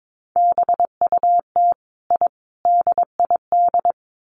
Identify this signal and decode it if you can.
Morse code, heard as a keyed tone, about 22 wpm, 700 Hz. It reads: BUT I DID
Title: Sets of 5 Words Encore 22wpm